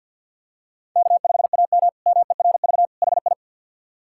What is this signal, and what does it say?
Morse code, heard as a keyed tone, about 53 wpm, 700 Hz. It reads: K5AM MERV HI
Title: K1USN SST QSO Practice - 50wpm